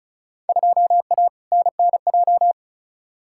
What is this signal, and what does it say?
Morse code, heard as a keyed tone, about 36 wpm, 700 Hz. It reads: 2A NNJ
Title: Field Day Exchanges - 35wpm